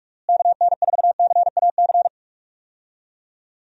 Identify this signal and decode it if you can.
Morse code, heard as a keyed tone, about 44 wpm, 700 Hz. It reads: KN4CAC
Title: USA Call Signs - 6 Characters - Repeated 2x - 45wpm